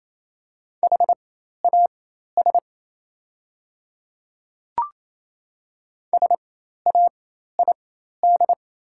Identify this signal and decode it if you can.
Morse code, heard as a keyed tone, about 28 wpm, spaced as for 13 wpm, 690 Hz. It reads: HAS SAID
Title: 4x Character Spacing - Top 100 Words - 28wpm